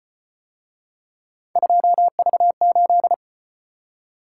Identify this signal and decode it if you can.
Morse code, heard as a keyed tone, about 34 wpm, 700 Hz. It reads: 2V8